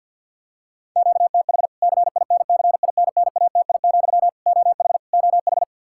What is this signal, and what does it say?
Morse code, heard as a keyed tone, about 51 wpm, 700 Hz. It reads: QTH CINCINNATI, OH OH
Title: K1USN SST QSO Practice 50wpm